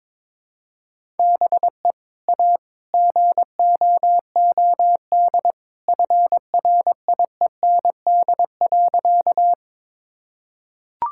Morse code, heard as a keyed tone, about 22 wpm, 700 Hz. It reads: BE A GOOD FRIEND.
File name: practice / 22wpm / Sentences with 2-Word Phrases - 22wpm